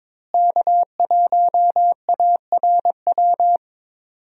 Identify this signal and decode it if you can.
Morse code, heard as a keyed tone, about 22 wpm, 700 Hz. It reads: K1ARW
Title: USA Call Signs - 5 Characters - Repeated 3x - 22wpm